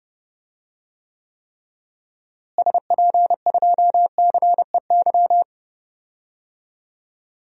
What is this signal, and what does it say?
Morse code, heard as a keyed tone, about 30 wpm, 700 Hz. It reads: SP2CEY
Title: DXCC-100 Call Signs - 6 Characters - Repeated 3x - 30wpm